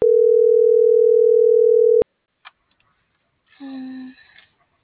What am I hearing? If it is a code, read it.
Ringback tone (North America), 440 + 480 Hz.